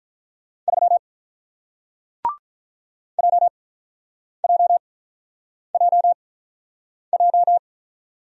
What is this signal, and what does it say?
Morse code, heard as a keyed tone, about 48 wpm, spaced as for 9 wpm, 700 Hz. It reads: 3 JJJJ